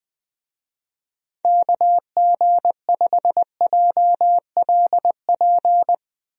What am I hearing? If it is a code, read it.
Morse code, heard as a keyed tone, about 20 wpm, 700 Hz. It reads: KG5JLP
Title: USA Call Signs - 6 Characters - 20wpm